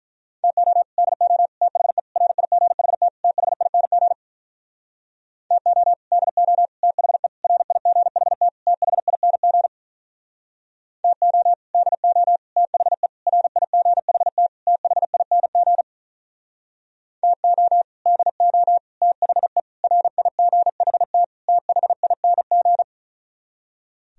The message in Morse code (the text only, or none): TO DO THE RIGHT THING TO DO THE RIGHT THING TO DO THE RIGHT THING TO DO THE RIGHT THING